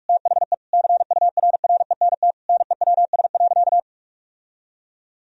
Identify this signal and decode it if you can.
Morse code, heard as a keyed tone, about 41 wpm, 700 Hz. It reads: THE CURRENT NEWS.
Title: Sentences from Top 1000 Words - Review Part 4 - 45wpm